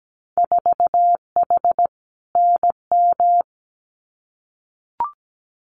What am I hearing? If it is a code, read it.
Morse code, heard as a keyed tone, about 17 wpm, 700 Hz. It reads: 4H NM
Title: Winter Field Day Exchanges Encore 17wpm